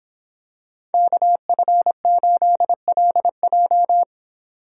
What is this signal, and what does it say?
Morse code, heard as a keyed tone, about 26 wpm, 700 Hz. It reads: KF8LJ